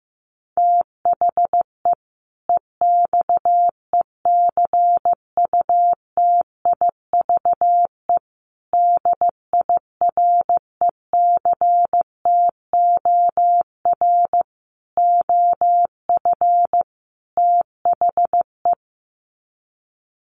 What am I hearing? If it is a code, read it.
Morse code, heard as a keyed tone, about 15 wpm, 700 Hz. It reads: THE EXECUTIVE DIRECTOR OF THE
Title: Sets of 5 Words Encore 15wpm